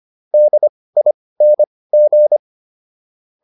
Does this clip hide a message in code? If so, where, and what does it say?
Yes: Morse code from 0.3 to 2.4 s, heard as a keyed tone, about 25 wpm, spaced as for 18 wpm, 600 Hz. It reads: DING